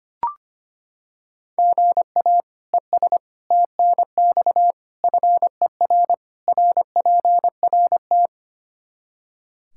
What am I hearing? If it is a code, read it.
Morse code, heard as a keyed tone, about 25 wpm, 700 Hz. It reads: GA ES TNX FER RPRT
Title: Practice QSO - Exchange 1 - Beginning - 25wpm